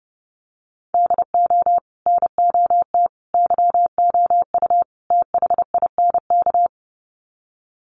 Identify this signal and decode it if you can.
Morse code, heard as a keyed tone, about 30 wpm, 700 Hz. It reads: DO NOT YOU THINK
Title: Sets of 4 Words 30wpm